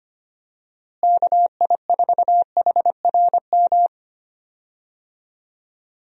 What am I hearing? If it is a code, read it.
Morse code, heard as a keyed tone, about 25 wpm, 700 Hz. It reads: KI4HRM
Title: USA Call Signs - 6 Characters - Repeated 3x - 25wpm